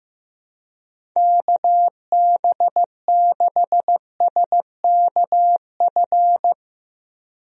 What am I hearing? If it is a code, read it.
Morse code, heard as a keyed tone, about 15 wpm, 700 Hz. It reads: KB6SKF
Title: USA Call Signs - 6 Characters - Repeated 2x - 15wpm